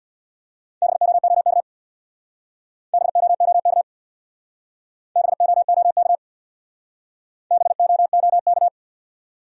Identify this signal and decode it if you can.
Morse code, heard as a keyed tone, about 60 wpm, 700 Hz. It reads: BOOK BOOK BOOK BOOK
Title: Speed-Racing Top 300 Words 50wpm